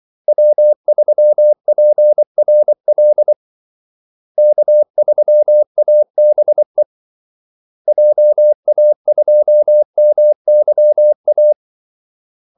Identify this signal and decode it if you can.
Morse code, heard as a keyed tone, about 24 wpm, 600 Hz. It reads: W3PRL K3ABE JA2MYA
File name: Callsigns-24wpm